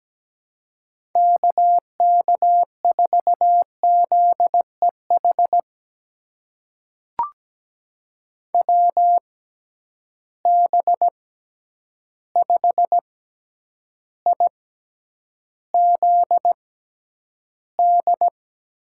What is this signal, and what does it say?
Morse code, heard as a keyed tone, about 17 wpm, spaced as for 9 wpm, 700 Hz. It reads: KK4ZEH WB5IZD
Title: USA Call Signs - 6 Characters - 6x Character Spacing - 17wpm